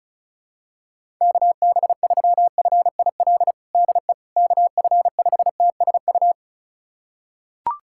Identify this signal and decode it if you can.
Morse code, heard as a keyed tone, about 34 wpm, 700 Hz. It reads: KB3FIL DE KF5TSU
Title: USA Callsign DE Callsign - 6 Characters - 35wpm